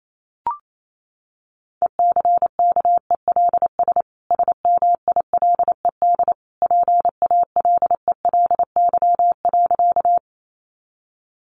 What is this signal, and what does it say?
Morse code, heard as a keyed tone, about 28 wpm, 700 Hz. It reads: ECKELS SMILED PALELY.
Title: A Sound Of Thunder 28wpm